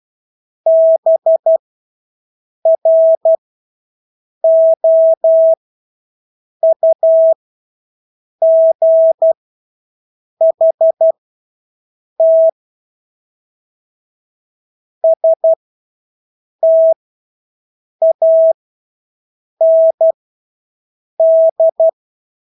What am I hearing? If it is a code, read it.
Morse code, heard as a keyed tone, about 12 wpm, spaced as for 6 wpm, 650 Hz. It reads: BROUGHT STAND
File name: Engelske ord 12-6 wpm